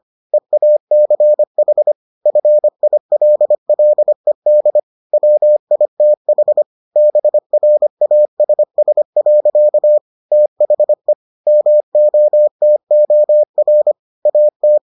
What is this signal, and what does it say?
Morse code, heard as a keyed tone, about 25 wpm, 600 Hz. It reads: EACH FILLED WITH BRASS. THE MOTOR AT